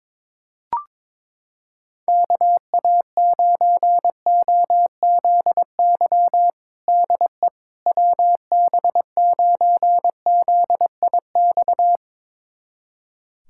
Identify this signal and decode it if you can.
Morse code, heard as a keyed tone, about 22 wpm, 700 Hz. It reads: KA9OZY DE WB9ZIX
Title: USA Callsign DE Callsign - 6 Characters - 22wpm